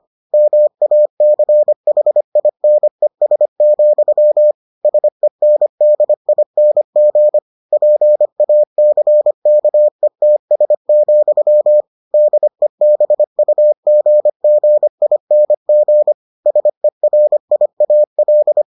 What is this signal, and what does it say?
Morse code, heard as a keyed tone, about 25 wpm, 600 Hz. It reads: MACHINES, SENDING PACKETS, DEBUGGING SERIAL